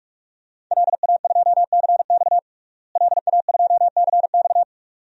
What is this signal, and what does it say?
Morse code, heard as a keyed tone, about 43 wpm, 700 Hz. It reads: LA2CX LA2CX
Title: DXCC-100 Call Signs - 5 Characters - Repeated 2x - 45wpm